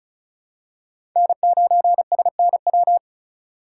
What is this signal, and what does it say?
Morse code, heard as a keyed tone, about 34 wpm, 700 Hz. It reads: N9SNW
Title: USA Call Signs - 5 Characters - 8x Character Spacing - 35wpm